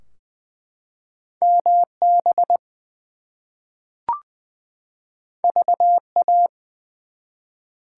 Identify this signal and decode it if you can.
Morse code, heard as a keyed tone, about 20 wpm, 700 Hz. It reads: MB VA
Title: US States and Canadian Provinces 20wpm